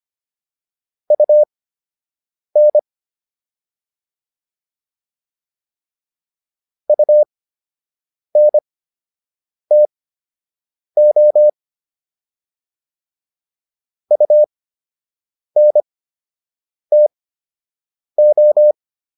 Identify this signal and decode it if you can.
Morse code, heard as a keyed tone, about 25 wpm, spaced as for 7 wpm, 600 Hz. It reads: UN UNTO UNTO